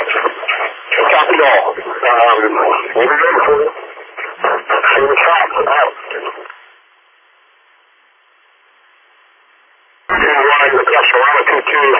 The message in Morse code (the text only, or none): none